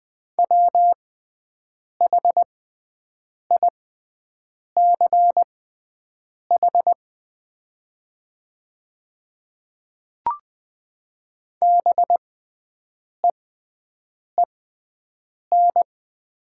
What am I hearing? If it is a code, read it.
Morse code, heard as a keyed tone, about 20 wpm, spaced as for 7 wpm, 690 Hz. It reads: WHICH BEEN
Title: 6x Character Spacing - Top 100 Words - 20wpm